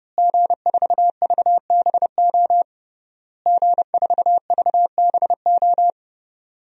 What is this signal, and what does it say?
Morse code, heard as a keyed tone, about 30 wpm, 700 Hz. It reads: G4VBO G4VBO
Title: DXCC-100 Call Signs - 5 Characters - Repeated 3x - 30wpm